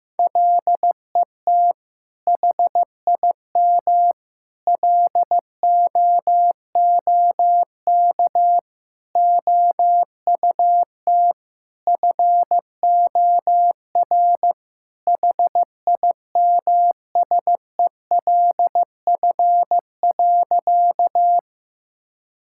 Morse code, heard as a keyed tone, about 15 wpm, 700 Hz. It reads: LET HIM LOOK OUT FOR HIMSELF.
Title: Sentences from Top 1600 Words 15wpm